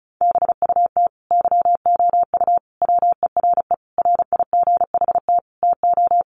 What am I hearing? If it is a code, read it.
Morse code, heard as a keyed tone, about 34 wpm, 700 Hz. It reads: BUT YOU WERE RIGHT TO
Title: Sentences from Top 800 Words 35wpm